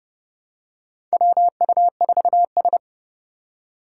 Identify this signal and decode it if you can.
Morse code, heard as a keyed tone, about 31 wpm, 700 Hz. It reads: WU4S
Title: USA Call Signs - 4 Characters - 2x Character Spacing - 30wpm